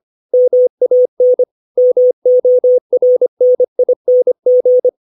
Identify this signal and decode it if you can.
Morse code, heard as a keyed tone, about 25 wpm, 500 Hz. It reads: MAN MORNING
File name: 632 words / 632 Words 25WPM